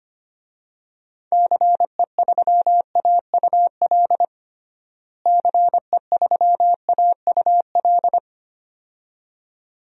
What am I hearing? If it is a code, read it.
Morse code, heard as a keyed tone, about 25 wpm, 700 Hz. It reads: CE3AUL CE3AUL
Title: DXCC-100 Call Signs - 6 Characters - Repeated 2x - 25wpm